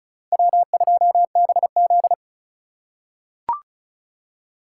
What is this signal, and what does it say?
Morse code, heard as a keyed tone, about 34 wpm, 690 Hz. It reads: W2BZ
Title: USA Call Signs - 4x Character Spacing - 35wpm